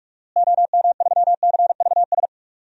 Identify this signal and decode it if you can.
Morse code, heard as a keyed tone, about 43 wpm, 700 Hz. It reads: OM3CVS
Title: DXCC-100 Call Signs - 6 Characters - Repeated 3x - 45wpm